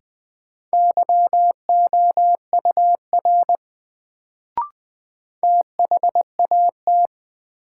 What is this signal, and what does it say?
Morse code, heard as a keyed tone, about 20 wpm, 700 Hz. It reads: YOUR THAT
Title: Sets of 2 Words - New Word Drill - 20wpm